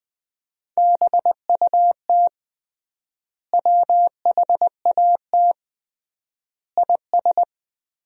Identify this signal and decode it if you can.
Morse code, heard as a keyed tone, about 21 wpm, 700 Hz. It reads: BUT WHAT IS
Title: 3x Word-Spacing - Sentences from Top 300 Words - 20wpm